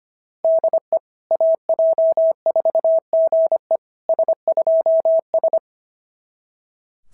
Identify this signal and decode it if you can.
Morse code, heard as a keyed tone, about 22 wpm, 650 Hz. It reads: DE AJ4GE S2S
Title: SOTA QSO Practice 25wpm